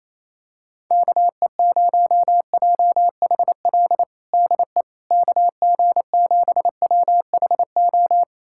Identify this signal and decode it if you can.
Morse code, heard as a keyed tone, about 28 wpm, 700 Hz. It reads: KE0JHL DE KG7WHO
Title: USA Callsign DE Callsign 28wpm